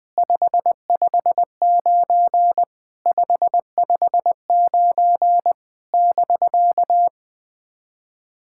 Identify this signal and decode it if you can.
Morse code, heard as a keyed tone, about 21 wpm, 700 Hz. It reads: 559 559 <BK>
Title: SOTA QSO Practice 20wpm